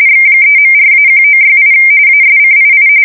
RTTY, about 45.45 baud. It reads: X3.4-1968 ASCII
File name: fsk170 / teletype-corp